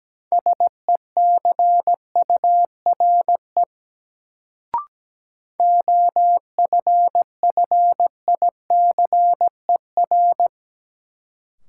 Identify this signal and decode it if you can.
Morse code, heard as a keyed tone, about 17 wpm, 700 Hz. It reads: SECURE OFFICER
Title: Sentences from Top 1200 Words - New Word Drill Alt - 17wpm